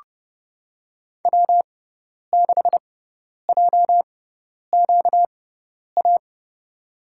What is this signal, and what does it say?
Morse code, heard as a keyed tone, about 32 wpm, spaced as for 10 wpm, 700 Hz. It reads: W6JQA
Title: USA Call Signs - 5 Characters - 6x Character Spacing - 30wpm